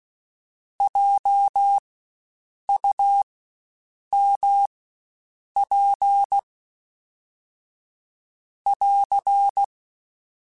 Morse code, heard as a keyed tone, about 16 wpm, spaced as for 7 wpm, 800 Hz. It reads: JUMP <AR>
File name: Spaced Alphas / Spaced Alphas mp3s / Js sp @15